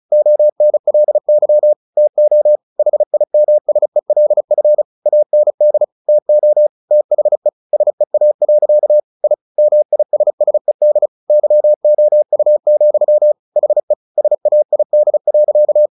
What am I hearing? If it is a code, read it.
Morse code, heard as a keyed tone, about 34 wpm, 600 Hz. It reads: ONLY TO HIMSELF AND TO THE SEA. I MISSED YOU, HE SAID.
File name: the old man and the sea 2 35wpm